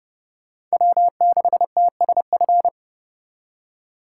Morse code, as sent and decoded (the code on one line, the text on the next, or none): .-- -.... - ... ..-.
W6TSF